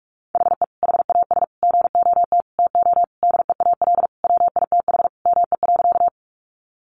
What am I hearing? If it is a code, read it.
Morse code, heard as a keyed tone, about 44 wpm, 700 Hz. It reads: HE HAS GOT TO DEAL WITH ME.
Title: Sentences from Top 800 Words 45wpm